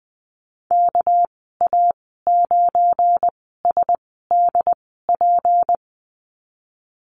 Morse code, heard as a keyed tone, about 20 wpm, spaced as for 15 wpm, 700 Hz. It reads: KA9SDP
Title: 219 Lesson 35g - Number 0 - Call Signs 20wpm